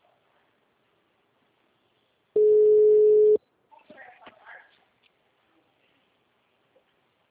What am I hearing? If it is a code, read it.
Ringback tone (Europe, 425 Hz), with speech.
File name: out- -2015- - - .4379